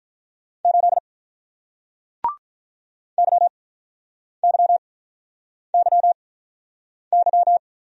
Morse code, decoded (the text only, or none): 8 YYYY